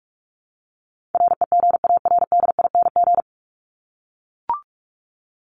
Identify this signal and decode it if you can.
Morse code, heard as a keyed tone, about 44 wpm, 700 Hz. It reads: REGARDING
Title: ING Suffix - Top 100 Words with Base Words 45wpm